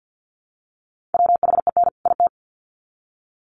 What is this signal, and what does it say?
Morse code, heard as a keyed tone, about 50 wpm, 700 Hz. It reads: WHEN IT
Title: Sets of 2 Words 50wpm